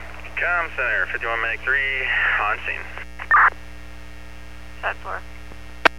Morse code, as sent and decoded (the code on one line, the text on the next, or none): none